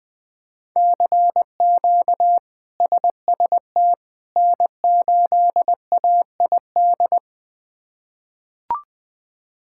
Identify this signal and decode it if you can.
Morse code, heard as a keyed tone, about 20 wpm, 700 Hz. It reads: CQ SST N8AID
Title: K1USN SST QSO Practice - 20wpm